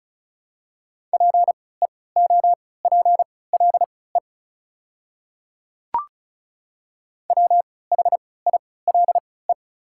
Morse code, heard as a keyed tone, about 34 wpm, spaced as for 20 wpm, 700 Hz. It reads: PEOPLE WHILE
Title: 3x Character Spacing - Top 300 Words - 35wpm